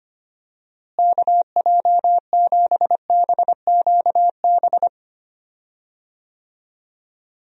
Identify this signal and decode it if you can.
Morse code, heard as a keyed tone, about 25 wpm, 700 Hz. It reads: KJ7BQB
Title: USA Call Signs - 6 Characters - Repeated 3x - 25wpm